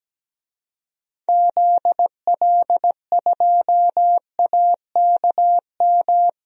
Morse code, heard as a keyed tone, about 17 wpm, 700 Hz. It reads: ZL2AKM
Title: DXCC-100 Call Signs - 6 Characters - 17wpm